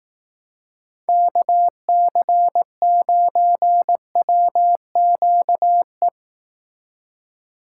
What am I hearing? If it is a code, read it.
Morse code, heard as a keyed tone, about 18 wpm, 700 Hz. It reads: KC9WQE
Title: Speed Racing - USA 2x3 Call Signs 15wpm